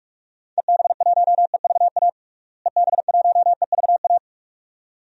Morse code, heard as a keyed tone, about 47 wpm, 700 Hz. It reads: EB1EVA EB1EVA
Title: DXCC-100 Call Signs - 6 Characters - Repeated 2x - 45wpm